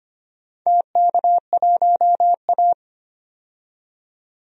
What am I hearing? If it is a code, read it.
Morse code, heard as a keyed tone, about 25 wpm, 700 Hz. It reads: TK1A
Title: DXCC-100 Call Signs - 4 Characters - 25wpm